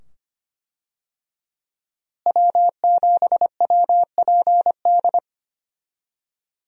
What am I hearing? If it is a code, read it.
Morse code, heard as a keyed tone, about 25 wpm, 700 Hz. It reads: W7WPD
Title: USA Call Signs - 5 Characters - Repeated 2x - 25wpm